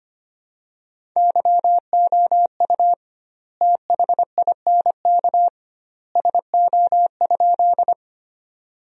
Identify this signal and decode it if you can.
Morse code, heard as a keyed tone, about 25 wpm, 700 Hz. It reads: YOU THINK SO?
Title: 2x Word-Spacing - Sentences from Top 200 Words - 25wpm